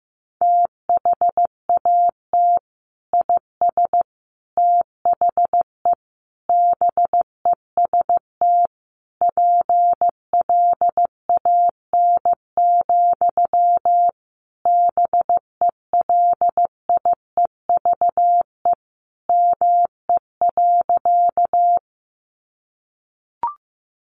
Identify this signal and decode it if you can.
Morse code, heard as a keyed tone, about 15 wpm, 700 Hz. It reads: THAT IS THE BEST PLAN, BELIEVE ME.
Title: Sentences from Top 1000 Words 15wpm